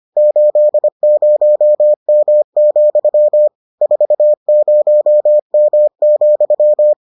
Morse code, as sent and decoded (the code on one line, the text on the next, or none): ---.. ----- -- --..-- / ....- ----- -- --..--
80M, 40M,